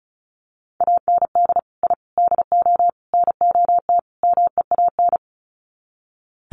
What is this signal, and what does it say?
Morse code, heard as a keyed tone, about 34 wpm, 700 Hz. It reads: AND I DO NOT MEAN
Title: Sets of 5 Words Encore2 35wpm